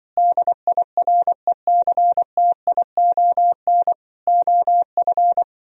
Morse code, heard as a keyed tone, about 24 wpm, 700 Hz. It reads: DIRECTION OF
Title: Speed-Racing - Sets of 4 Words 20wpm